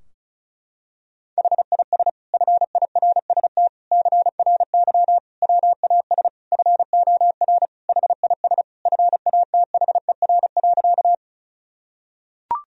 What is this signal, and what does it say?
Morse code, heard as a keyed tone, about 36 wpm, 700 Hz. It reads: HIS FIRST CRY WAS FOR HIS FATHER.
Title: Sentences from Top 1000 Words - Review Part 2 - 35wpm